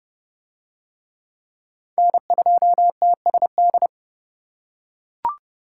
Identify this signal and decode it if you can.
Morse code, heard as a keyed tone, about 29 wpm, 700 Hz. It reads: N2TSD